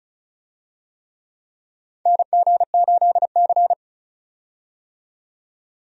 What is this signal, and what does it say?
Morse code, heard as a keyed tone, about 36 wpm, 700 Hz. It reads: NG8C
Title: USA Call Signs - 4 Characters - 35wpm